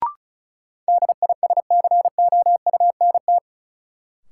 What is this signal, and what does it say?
Morse code, heard as a keyed tone, about 36 wpm, 700 Hz. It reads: DISCOUNT
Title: Sentences from Top 1200 Words - New Word Drill - 35wpm